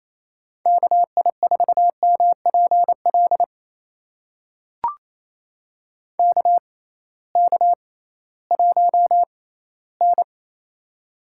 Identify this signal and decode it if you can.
Morse code, heard as a keyed tone, about 28 wpm, spaced as for 16 wpm, 700 Hz. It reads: KI4MPL KK1N
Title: USA Call Signs - 6x Character Spacing - 28wpm